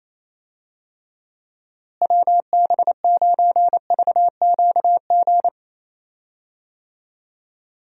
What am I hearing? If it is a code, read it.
Morse code, heard as a keyed tone, about 27 wpm, 700 Hz. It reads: WB9VQG
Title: USA Call Signs - 6 Characters - Repeated 2x - 28wpm